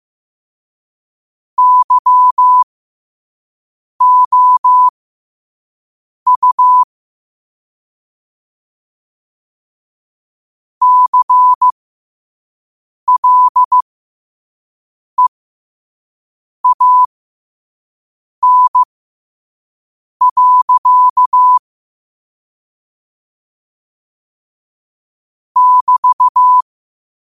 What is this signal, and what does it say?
Morse code, heard as a keyed tone, about 15 wpm, spaced as for 5 wpm, 1000 Hz. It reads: YOU CLEAN. <BT>
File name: qotdfiles / 5wpm